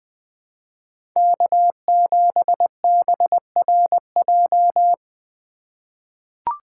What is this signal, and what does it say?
Morse code, heard as a keyed tone, about 20 wpm, 700 Hz. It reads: K7BRJ
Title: USA Call Signs - 2x Character Spacing - 20wpm